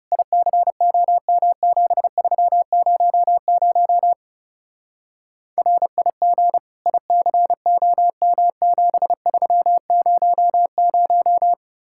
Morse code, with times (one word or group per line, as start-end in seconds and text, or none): 0.1-4.1 ICOM7300
5.6-6.6 RIG
6.9-11.5 ICOM7300